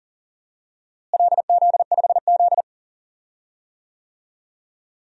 Morse code, heard as a keyed tone, about 39 wpm, 700 Hz. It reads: LZ5Z
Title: DXCC-100 Call Signs - 4 Characters - Repeated 3x - 40wpm